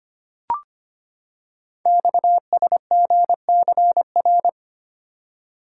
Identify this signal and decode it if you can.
Morse code, heard as a keyed tone, about 25 wpm, 690 Hz. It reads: XSGCR
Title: Code Groups - 5 Letters - 25wpm